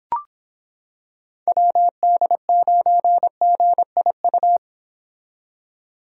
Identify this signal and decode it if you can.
Morse code, heard as a keyed tone, about 26 wpm, 700 Hz. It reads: WD9GIU